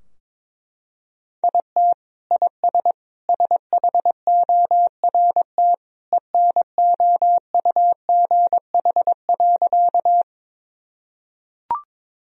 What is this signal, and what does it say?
Morse code, heard as a keyed tone, about 22 wpm, 700 Hz. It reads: IT IS SHORT ENOUGH.
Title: Sentences from Top 1000 Words - Review Part 2 - 22wpm